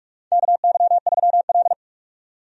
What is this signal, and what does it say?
Morse code, heard as a keyed tone, about 46 wpm, 700 Hz. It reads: KY3L
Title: USA Call Signs - 4 Characters - 45wpm